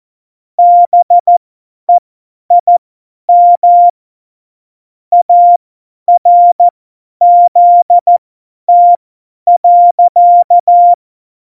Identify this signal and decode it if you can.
Morse code, heard as a keyed tone, about 14 wpm, spaced as for 10 wpm, 700 Hz. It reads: BEIM ARZT.